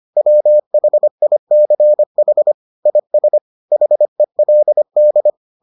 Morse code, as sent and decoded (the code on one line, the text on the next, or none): .-- .... .. -.-. .... / .. ... / .... .-.. -..
WHICH IS HELD